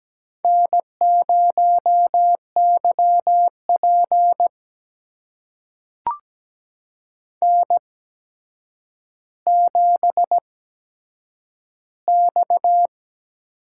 Morse code, heard as a keyed tone, about 17 wpm, spaced as for 8 wpm, 700 Hz. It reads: N0YP N7X
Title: USA Call Signs - 4 Characters - 8x Character Spacing - 17wpm